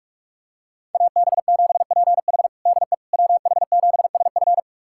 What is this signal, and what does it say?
Morse code, heard as a keyed tone, about 41 wpm, 700 Hz. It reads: AB7PH DE WH7SF
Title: USA Callsign DE Callsign - 5 Characters - 45wpm